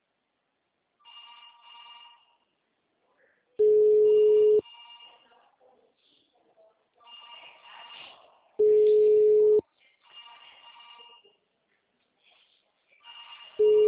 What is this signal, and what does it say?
Ringback tone (Europe, 425 Hz), with speech.
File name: out- -2008- - - .64246